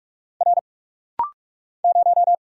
Morse code, heard as a keyed tone, about 45 wpm, 700 Hz. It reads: R 0
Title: Single Letter-Number Alt - Rapid-Fire 45wpm